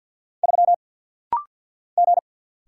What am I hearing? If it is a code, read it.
Morse code, heard as a keyed tone, about 48 wpm, 700 Hz. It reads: 3 G
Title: Single Letter-Number - Rapid-Fire 50wpm